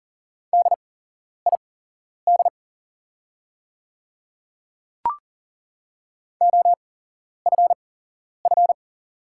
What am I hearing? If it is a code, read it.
Morse code, heard as a keyed tone, about 40 wpm, spaced as for 11 wpm, 690 Hz. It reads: DID OFF
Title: 8x Character Spacing - Top 200 Words - 40wpm